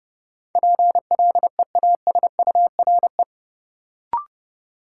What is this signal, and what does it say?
Morse code, heard as a keyed tone, about 30 wpm, 700 Hz. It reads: PLEASURE
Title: Sentences from Top 1600 Words - New Word Drill Alt - 30wpm